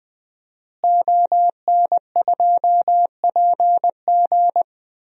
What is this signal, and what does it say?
Morse code, heard as a keyed tone, about 20 wpm, 700 Hz. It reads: ON2PG
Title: DXCC-100 Call Signs - 5 Characters - Repeated 3x - 20wpm